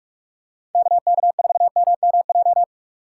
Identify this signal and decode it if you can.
Morse code, heard as a keyed tone, about 46 wpm, 690 Hz. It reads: KK4MMJ
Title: USA Call Signs - 3x Character Spacing - 45wpm